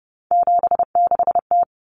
Morse code, heard as a keyed tone, about 30 wpm, 700 Hz. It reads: 76T
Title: Common Cut Numbers - 3-Digit 30wpm